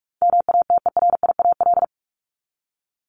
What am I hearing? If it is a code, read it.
Morse code, heard as a keyed tone, about 44 wpm, 700 Hz. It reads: MATERIAL
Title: ES:S Suffix - Top 100 Words with Base Words 45wpm